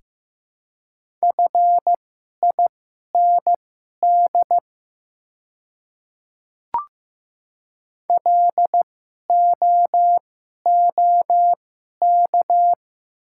Morse code, heard as a keyed tone, about 15 wpm, spaced as for 11 wpm, 700 Hz. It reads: FIND LOOK
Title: 2x Character Spacing - Top 200 Words - 15wpm